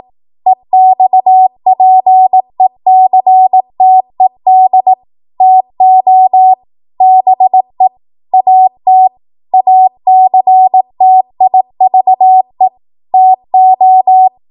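Morse code, heard as a keyed tone, about 18 wpm, 750 Hz. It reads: EXPECTED TO BE AT ACTIVE TO
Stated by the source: 18 WPM Morse code practice from a national amateur radio association's code archive.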